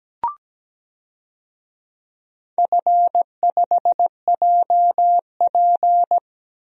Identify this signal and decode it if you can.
Morse code, heard as a keyed tone, about 17 wpm, 700 Hz. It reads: F5JP
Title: DXCC-100 Call Signs - 4 Characters - Repeated 2x - 17wpm